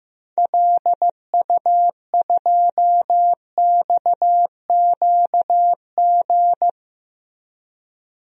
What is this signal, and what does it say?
Morse code, heard as a keyed tone, about 16 wpm, 700 Hz. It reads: LU2XQG
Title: DXCC-100 Call Signs - 6 Characters - 15wpm